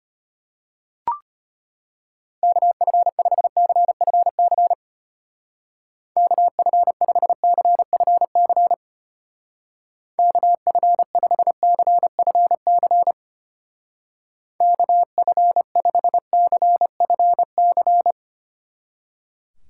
Morse code, heard as a keyed tone, about 32 wpm, 700 Hz. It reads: KF5CFC KF5CFC KF5CFC KF5CFC